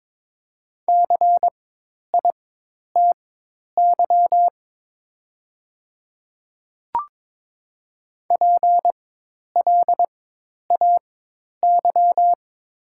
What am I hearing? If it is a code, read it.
Morse code, heard as a keyed tone, about 21 wpm, spaced as for 10 wpm, 700 Hz. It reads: CITY PLAY
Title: 4x Character Spacing - Top 300 Words - 22wpm